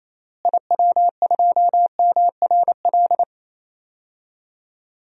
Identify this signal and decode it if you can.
Morse code, heard as a keyed tone, about 28 wpm, 700 Hz. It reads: IW2MRL
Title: DXCC-100 Call Signs - 6 Characters - Repeated 3x - 28wpm